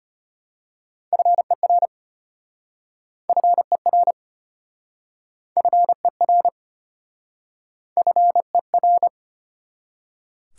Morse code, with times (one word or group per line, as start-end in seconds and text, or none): 1.1-1.9 FER
3.3-4.1 FER
5.6-6.5 FER
8.0-9.1 FER